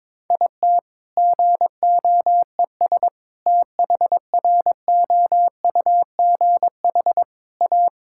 Morse code, heard as a keyed tone, about 22 wpm, 700 Hz. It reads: IT GOES THROUGH A
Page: morse-code-ninja-practice-22wpm